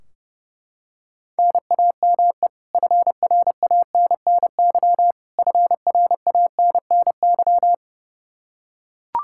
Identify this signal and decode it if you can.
Morse code, heard as a keyed tone, about 30 wpm, 700 Hz. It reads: NAME FRANNY FRANNY
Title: Practice QSO - Exchange 1 - Beginning - 30wpm